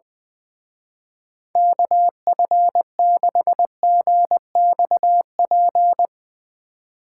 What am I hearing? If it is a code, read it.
Morse code, heard as a keyed tone, about 22 wpm, 700 Hz. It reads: KF6GXP
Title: Speed Racing - USA Call Signs 20wpm